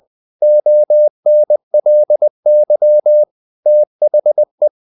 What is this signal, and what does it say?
Morse code, heard as a keyed tone, about 20 wpm, 600 Hz. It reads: ONLY THE
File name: the old man and the sea part 2 20wpm